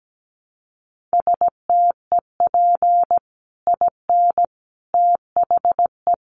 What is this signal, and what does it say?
Morse code, heard as a keyed tone, about 17 wpm, 700 Hz. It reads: STEP IN THE
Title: Sets of 5 Words Encore 17wpm